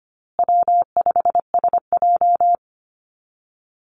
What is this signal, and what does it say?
Morse code, heard as a keyed tone, about 25 wpm, 700 Hz. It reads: W5SJ
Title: Call Signs Repeated 2x - 25wpm